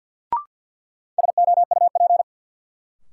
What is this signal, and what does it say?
Morse code, heard as a keyed tone, about 52 wpm, 700 Hz. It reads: SOUP
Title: Sentences from Top 1600 Words - New Word Drill Alt - 50wpm